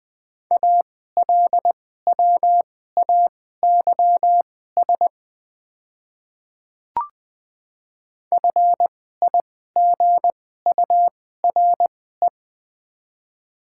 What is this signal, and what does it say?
Morse code, heard as a keyed tone, about 21 wpm, spaced as for 15 wpm, 700 Hz. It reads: ALWAYS FIGURE
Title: 2x Character Spacing - Top 500 Words - 20wpm